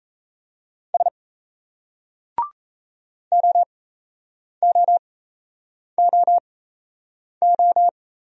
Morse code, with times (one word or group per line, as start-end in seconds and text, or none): 0.9-1.1 S
3.3-7.9 OOOO